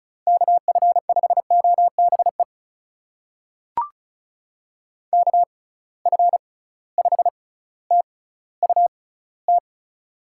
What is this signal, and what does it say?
Morse code, heard as a keyed tone, about 36 wpm, spaced as for 18 wpm, 700 Hz. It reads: KF5OBE KF5TUT
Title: USA Call Signs - 6 Characters - 6x Character Spacing - 35wpm